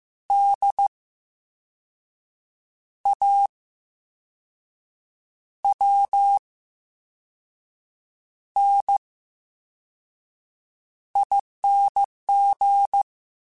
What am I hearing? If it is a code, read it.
Morse code, heard as a keyed tone, about 15 wpm, 800 Hz. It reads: D A W N I N G
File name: Special INGs-mp3s / Special INGs-mp3s / VerySlowToFaster Ing @13